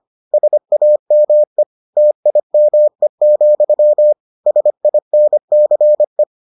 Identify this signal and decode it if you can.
Morse code, heard as a keyed tone, about 25 wpm, 600 Hz. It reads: SAME TIME, SINCE